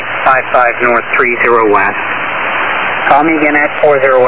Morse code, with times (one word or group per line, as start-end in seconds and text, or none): none